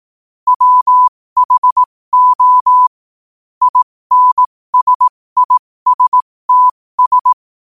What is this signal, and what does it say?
Morse code, heard as a keyed tone, about 18 wpm, spaced as for 15 wpm, 1000 Hz. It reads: WHO INSISTS
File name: qotdfiles / 15wpm